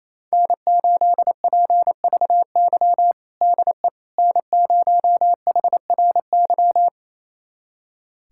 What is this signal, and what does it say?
Morse code, heard as a keyed tone, about 27 wpm, 700 Hz. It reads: N8PVY DE N0HRY
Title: USA Callsign DE Callsign - 5 Characters - 28wpm